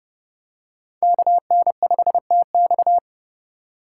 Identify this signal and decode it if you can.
Morse code, heard as a keyed tone, about 30 wpm, 700 Hz. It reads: KN5TX
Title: K1USN SST QSO Practice - 30wpm